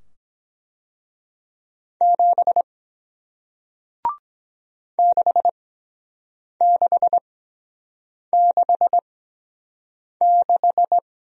Morse code, heard as a keyed tone, about 23 wpm, spaced as for 7 wpm, 700 Hz. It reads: 7 6666